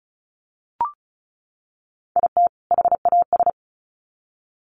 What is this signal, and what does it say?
Morse code, heard as a keyed tone, about 34 wpm, 700 Hz. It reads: IT HAS
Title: Sets of 2 Words 35wpm